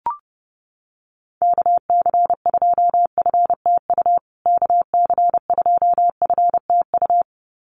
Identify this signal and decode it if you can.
Morse code, heard as a keyed tone, about 28 wpm, 700 Hz. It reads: KC2FTU KC2FTU
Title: USA Call Signs Repeated 2x 30wpm